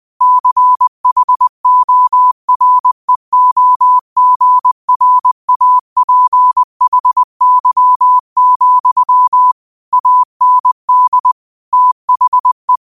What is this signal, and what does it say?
Morse code, heard as a keyed tone, about 20 wpm, 1000 Hz. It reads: CHOREOGRAPHY, AND THE
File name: 20wpm